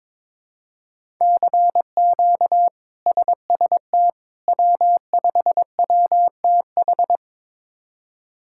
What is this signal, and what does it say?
Morse code, heard as a keyed tone, about 22 wpm, 700 Hz. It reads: CQ SST W5WTH
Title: K1USN SST QSO Practice - 22wpm